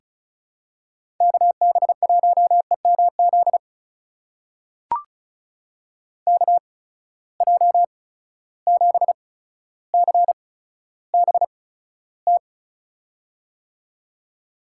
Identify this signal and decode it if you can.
Morse code, heard as a keyed tone, about 36 wpm, spaced as for 15 wpm, 700 Hz. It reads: KB1EMZ KJ7CBT
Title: USA Call Signs - 6 Characters - 8x Character Spacing - 35wpm